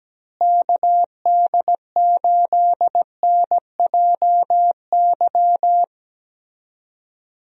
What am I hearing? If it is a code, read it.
Morse code, heard as a keyed tone, about 17 wpm, 700 Hz. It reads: KD8NJY
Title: USA Call Signs - 6 Characters - Repeated 2x - 17wpm